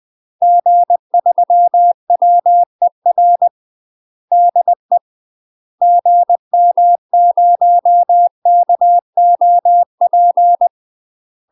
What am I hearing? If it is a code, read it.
Morse code, heard as a keyed tone, about 20 wpm, 700 Hz. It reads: G3WER DE GM0KOP
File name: UK MASTER QSOs TRAINING / 20WPM / UK MASTER QSOs 0025